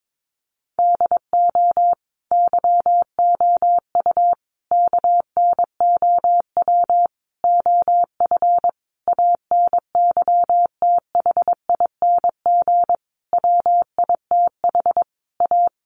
Morse code, heard as a keyed tone, about 22 wpm, 700 Hz. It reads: DO YOU KNOW OF ANYTHING WITH A
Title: Sentences from Top 1200 Words 22wpm